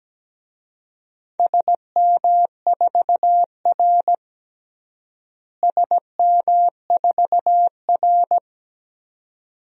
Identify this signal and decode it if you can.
Morse code, heard as a keyed tone, about 17 wpm, 700 Hz. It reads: SM4R SM4R
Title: DXCC-100 Call Signs - 4 Characters - Repeated 2x - 17wpm